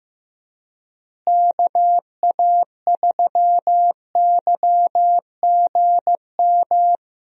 Morse code, heard as a keyed tone, about 15 wpm, 700 Hz. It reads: KA3YGM
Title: USA Call Signs - 6 Characters - Repeated 3x - 15wpm